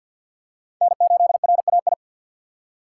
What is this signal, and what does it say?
Morse code, heard as a keyed tone, about 47 wpm, 700 Hz. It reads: N8RAI
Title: USA Call Signs - 6x Character Spacing - 50wpm